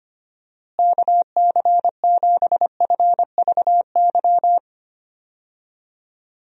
Morse code, heard as a keyed tone, about 25 wpm, 700 Hz. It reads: KC7FVY
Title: USA Call Signs - 6 Characters - Repeated 2x - 25wpm